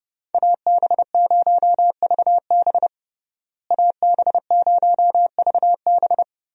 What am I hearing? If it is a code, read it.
Morse code, heard as a keyed tone, about 32 wpm, 700 Hz. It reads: AB0VB AB0VB
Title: USA Call Signs - 5 Characters - Repeated 2x - 30wpm